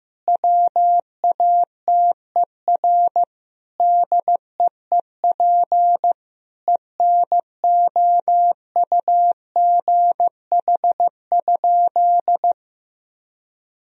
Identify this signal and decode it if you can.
Morse code, heard as a keyed tone, about 15 wpm, 700 Hz. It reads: WATER DEEP ENOUGH?
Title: Sentences from Top 1000 Words - Review Part 4 - 15wpm